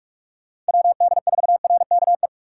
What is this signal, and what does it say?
Morse code, heard as a keyed tone, about 47 wpm, 700 Hz. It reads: WD4RKE
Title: USA Call Signs - 6 Characters - 2x Character Spacing - 45wpm